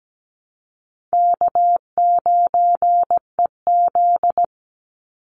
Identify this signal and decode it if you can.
Morse code, heard as a keyed tone, about 17 wpm, 700 Hz. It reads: K9EZ
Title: NA SOTA Chaser Callsigns 17wpm